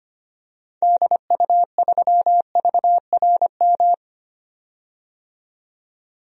Morse code, heard as a keyed tone, about 25 wpm, 700 Hz. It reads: DU3VRM
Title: DXCC-100 Call Signs - 6 Characters - 25wpm